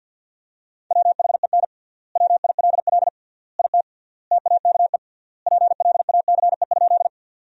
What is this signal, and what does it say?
Morse code, heard as a keyed tone, about 49 wpm, 700 Hz. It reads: WHEN WILL IT TAKE PLACE?
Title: 3x Word-Spacing - Sentences from Top 200 Words - 50wpm